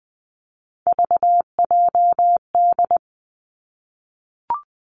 Morse code, heard as a keyed tone, about 20 wpm, 700 Hz. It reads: VJD
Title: Three Challenging Letter-Number Combinations 20wpm